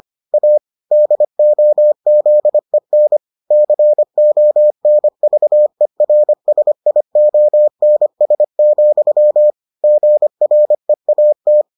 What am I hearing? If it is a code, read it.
Morse code, heard as a keyed tone, about 25 wpm, 600 Hz. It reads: A DOZEN CONVERSIONS, GREAT